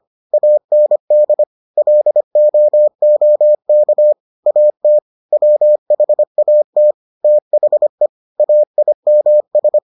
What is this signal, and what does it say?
Morse code, heard as a keyed tone, about 25 wpm, 600 Hz. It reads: AND LOOK AT WHAT THE AIMS